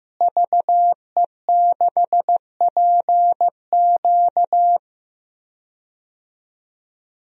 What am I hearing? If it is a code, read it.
Morse code, heard as a keyed tone, about 15 wpm, 700 Hz. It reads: VE6PQ
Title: DXCC-100 Call Signs - 5 Characters - Repeated 3x - 15wpm